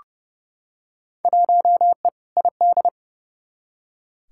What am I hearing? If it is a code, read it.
Morse code, heard as a keyed tone, about 30 wpm, 700 Hz. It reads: 1E ID
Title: Field Day Exchanges - 30wpm